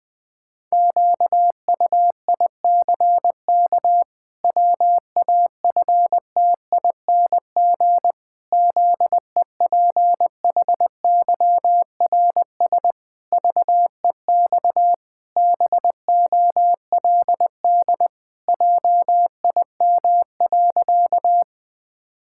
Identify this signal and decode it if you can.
Morse code, heard as a keyed tone, about 21 wpm, 700 Hz. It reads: QUICK WAFTING ZEPHYRS VEX BOLD JIM.
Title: Pangram Challenge v2 - 20wpm